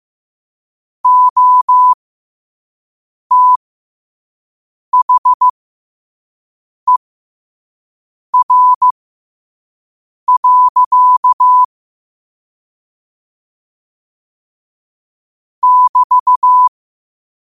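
Morse code, heard as a keyed tone, about 15 wpm, spaced as for 5 wpm, 1000 Hz. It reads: OTHER. <BT>
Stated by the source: Morse code practice at 5 wpm.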